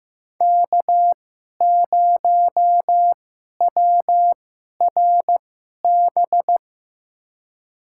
Morse code, heard as a keyed tone, about 15 wpm, spaced as for 11 wpm, 700 Hz. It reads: K0WRB
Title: USA Call Signs - 5 Characters - 2x Character Spacing - 15wpm